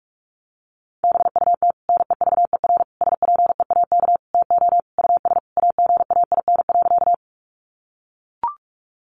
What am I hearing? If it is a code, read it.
Morse code, heard as a keyed tone, about 44 wpm, 700 Hz. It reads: BUT NEVER SPEAK TO US AGAIN.
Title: Sentences from Top 1100 Words 45wpm